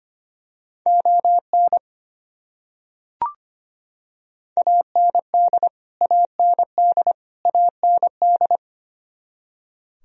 Morse code, heard as a keyed tone, about 25 wpm, 700 Hz. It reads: ON AND AND AND
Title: Top 100 Words Repeated 25wpm